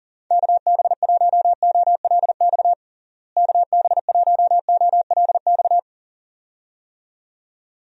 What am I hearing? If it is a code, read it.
Morse code, heard as a keyed tone, about 37 wpm, 700 Hz. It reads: KB1OLX KB1OLX
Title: USA Call Signs - 6 Characters - Repeated 2x - 40wpm